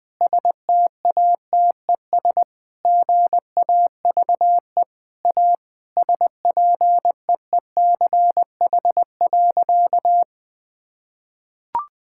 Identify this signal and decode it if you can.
Morse code, heard as a keyed tone, about 21 wpm, 700 Hz. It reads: STATES GAVE A SPEECH.
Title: Sentences with 5-Word Phrases - 20wpm